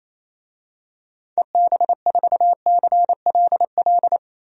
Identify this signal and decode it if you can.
Morse code, heard as a keyed tone, about 28 wpm, 700 Hz. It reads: EB4CLL
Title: DXCC-100 Call Signs - 6 Characters - Repeated 3x - 28wpm